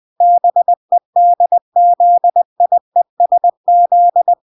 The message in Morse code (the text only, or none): BEDZIESZ